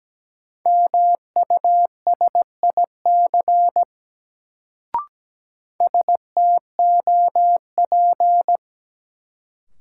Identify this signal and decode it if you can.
Morse code, heard as a keyed tone, about 17 wpm, 700 Hz. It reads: MUSIC STOP
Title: Sentences from Top 300 Words - New Word Drill - 17wpm